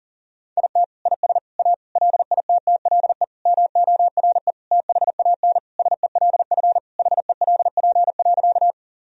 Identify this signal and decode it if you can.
Morse code, heard as a keyed tone, about 40 wpm, 700 Hz. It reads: IT IS A LITTLE MORE THAN SELF HELP.
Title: Sentences from Top 1000 Words - Review Part 1 - 40wpm